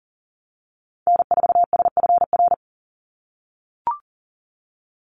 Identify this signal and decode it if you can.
Morse code, heard as a keyed tone, about 40 wpm, 700 Hz. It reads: N4SFR
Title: NA SOTA Activator Callsigns 40wpm